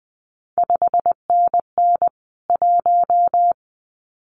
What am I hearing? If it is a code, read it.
Morse code, heard as a keyed tone, about 19 wpm, 700 Hz. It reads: 5NN 1
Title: CQ WW DX Contest Exchange Encore - 20wpm